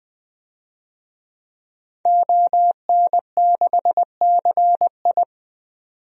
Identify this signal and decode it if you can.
Morse code, heard as a keyed tone, about 20 wpm, 700 Hz. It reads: ON6CI
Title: DXCC-100 Call Signs - 5 Characters - 20wpm